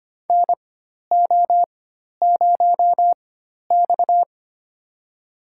Morse code, heard as a keyed tone, about 25 wpm, spaced as for 12 wpm, 700 Hz. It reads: NO0X
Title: USA Call Signs - 4 Characters - 4x Character Spacing - 25wpm